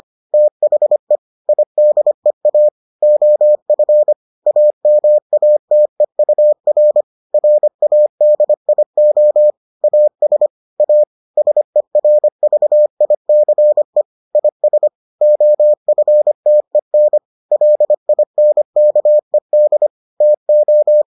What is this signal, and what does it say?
Morse code, heard as a keyed tone, about 25 wpm, 600 Hz. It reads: THE IDEA OF AMATEUR RADIO AS A SERVICE IS OFTEN LINKED TO